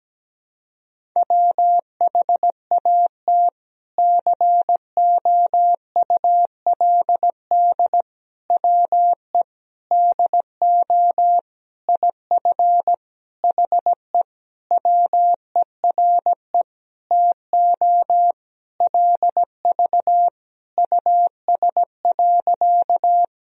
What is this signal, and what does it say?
Morse code, heard as a keyed tone, about 17 wpm, 700 Hz. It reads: WHAT COULD WE DO IF HE WERE TO LV US.
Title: Sentences from Top 400 Words and Abbreviations 17wpm